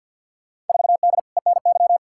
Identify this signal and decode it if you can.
Morse code, heard as a keyed tone, about 53 wpm, 700 Hz. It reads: 4D ENY
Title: Field Day Exchanges Encore - 50wpm